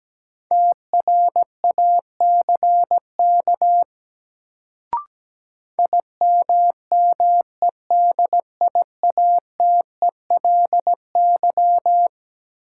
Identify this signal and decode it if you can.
Morse code, heard as a keyed tone, about 17 wpm, 700 Hz. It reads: TRACK IMMEDIATELY
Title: Sentences from Top 1100 Words - New Word Drill - 17wpm